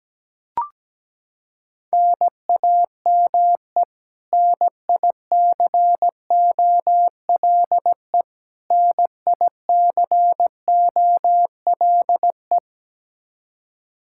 Morse code, heard as a keyed tone, about 17 wpm, 700 Hz. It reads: NAME NICOLE NICOLE
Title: Practice QSO - Exchange 1 - Beginning - 17wpm